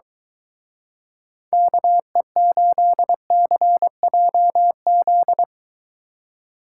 Morse code, heard as a keyed tone, about 23 wpm, 700 Hz. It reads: KE8CJZ